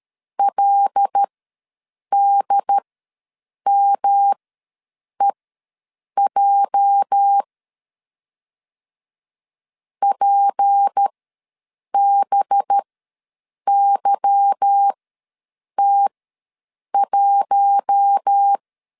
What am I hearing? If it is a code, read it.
Morse code, heard as a keyed tone, about 13 wpm, spaced as for 7 wpm, 800 Hz. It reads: LDMEJ PBYT1
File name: LEZIONI / 8 YJ45 / 8 YJ45 ws 12wpm 3146